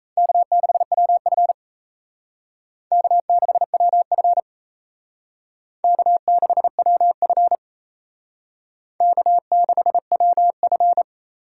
Morse code, heard as a keyed tone, about 35 wpm, 700 Hz. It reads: K6WF K6WF K6WF K6WF